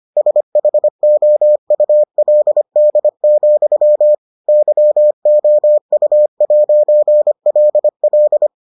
Morse code, heard as a keyed tone, about 25 wpm, 600 Hz. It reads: SHOULD, YOU'LL